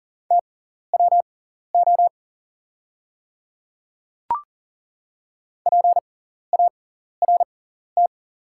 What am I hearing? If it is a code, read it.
Morse code, heard as a keyed tone, about 40 wpm, spaced as for 14 wpm, 690 Hz. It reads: TWO PART
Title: 6x Character Spacing - Top 200 Words - 40wpm